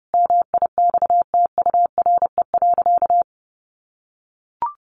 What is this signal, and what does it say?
Morse code, heard as a keyed tone, about 30 wpm, 700 Hz. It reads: MIXTURE.
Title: Sentences from Top 1500 Words 30wpm